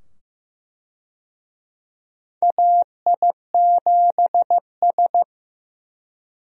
Morse code, heard as a keyed tone, about 15 wpm, 700 Hz. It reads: AI7S
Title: USA Call Signs - 4 Characters - 15wpm